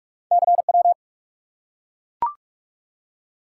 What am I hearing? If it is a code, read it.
Morse code, heard as a keyed tone, about 44 wpm, 700 Hz. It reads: CW
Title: Top 180 QSO Elements 45wpm